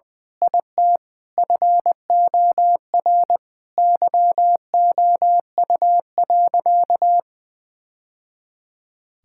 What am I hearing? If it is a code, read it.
Morse code, heard as a keyed tone, about 19 wpm, 700 Hz. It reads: IT FOR YOU.
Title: Sentences from Top 1000 Words - Review Part 6 - 20wpm